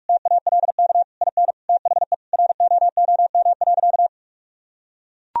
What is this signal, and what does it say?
Morse code, heard as a keyed tone, about 43 wpm, 700 Hz. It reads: TALK IN THE ROOM.
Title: Sentences from Top 1000 Words - Review Part 4 - 45wpm